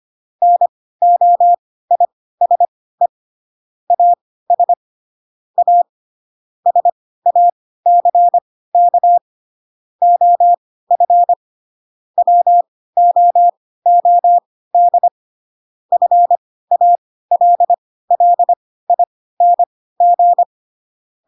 Morse code, heard as a keyed tone, about 25 wpm, spaced as for 16 wpm, 700 Hz. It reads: NOISE AS A SACK OF WOOD FALLING
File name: Pinocchio / Pinocchio ch7 10 25 ik1hge